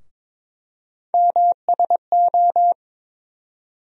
Morse code, heard as a keyed tone, about 22 wpm, 700 Hz. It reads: MSO
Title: USA Call Sign Postfixes - 3 Letters 22wpm